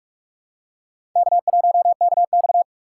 Morse code, heard as a keyed tone, about 46 wpm, 700 Hz. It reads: K1KX
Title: USA Call Signs - 4 Characters - 6x Character Spacing - 45wpm